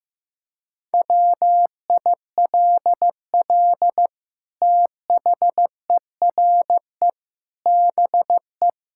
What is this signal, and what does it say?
Morse code, heard as a keyed tone, about 15 wpm, 700 Hz. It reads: WILL THERE BE
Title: Sentences from Top 1600 Words 15wpm